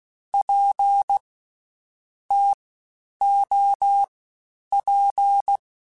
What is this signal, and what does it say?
Morse code, heard as a keyed tone, about 16 wpm, 800 Hz. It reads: P T O P